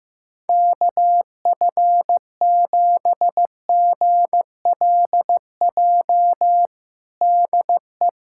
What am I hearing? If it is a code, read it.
Morse code, heard as a keyed tone, about 15 wpm, 700 Hz. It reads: KF7GLJ DE
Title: USA Callsign DE Callsign - 6 Characters - 15wpm